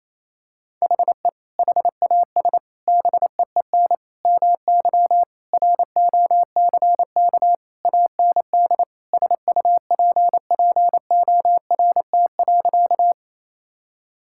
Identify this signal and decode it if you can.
Morse code, heard as a keyed tone, about 27 wpm, 700 Hz. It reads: HE HAS BEEN MY ROCK AND SUPPORT.
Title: Sentences from Top 1000 Words - Review Part 5 - 28wpm